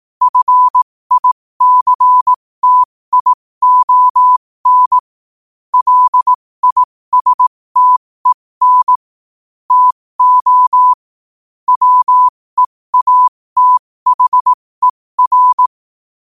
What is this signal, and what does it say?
Morse code, heard as a keyed tone, about 18 wpm, spaced as for 15 wpm, 1000 Hz. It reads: FICTION LISTEN TO WEATHER
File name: qotdfiles / 15wpm